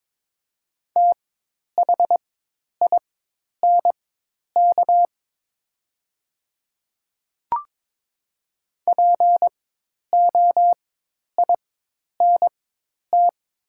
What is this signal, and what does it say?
Morse code, heard as a keyed tone, about 22 wpm, spaced as for 10 wpm, 700 Hz. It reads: THINK POINT
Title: 4x Character Spacing - Top 200 Words - 22wpm